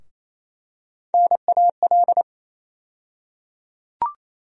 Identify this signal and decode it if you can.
Morse code, heard as a keyed tone, about 28 wpm, 700 Hz. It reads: NAL
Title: USA Call Sign Postfixes - 3 Letters 28wpm